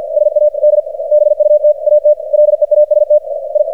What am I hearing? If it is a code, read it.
Morse code, heard as a keyed tone, about 47 wpm, 600 Hz, with radio noise. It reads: SAD BUT AT LEAST I
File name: archived / CRSnet